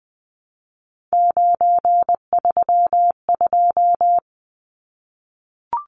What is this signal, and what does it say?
Morse code, heard as a keyed tone, about 20 wpm, 700 Hz. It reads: 932
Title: Numbers - 3-Digit 20wpm